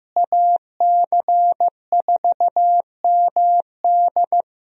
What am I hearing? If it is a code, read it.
Morse code, heard as a keyed tone, about 15 wpm, 700 Hz. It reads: AC4MD
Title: USA Call Signs - 5 Characters - Repeated 3x - 15wpm